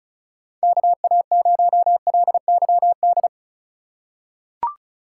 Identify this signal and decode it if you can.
Morse code, heard as a keyed tone, about 34 wpm, 700 Hz. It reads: KA0LYD
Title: USA Call Signs - 2x Character Spacing - 35wpm